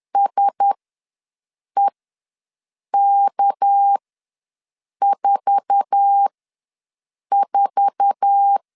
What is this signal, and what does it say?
Morse code, heard as a keyed tone, about 10 wpm, spaced as for 6 wpm, 800 Hz. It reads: SEK44
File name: LEZIONI / 8 YJ45 / 8 YJ4 ws 10wpm 4020